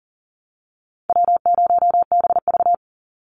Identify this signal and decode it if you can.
Morse code, heard as a keyed tone, about 40 wpm, 700 Hz. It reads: W0BV
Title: Easy Call Signs - 40wpm